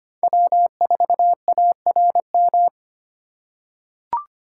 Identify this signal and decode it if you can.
Morse code, heard as a keyed tone, about 25 wpm, 700 Hz. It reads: W4ARM
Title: USA Call Signs - 5 Characters - 2x Character Spacing - 25wpm